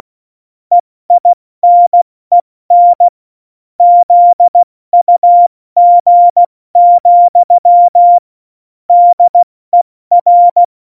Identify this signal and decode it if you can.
Morse code, heard as a keyed tone, about 16 wpm, 700 Hz. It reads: EINEN ZUG, DER
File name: Morsekurs / Witze / Witz 01. Dezember 2020, 70 F80